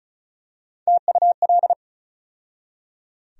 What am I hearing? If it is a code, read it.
Morse code, heard as a keyed tone, about 36 wpm, 700 Hz. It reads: TUL
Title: USA Call Sign Postfixes - 3 Letters 35wpm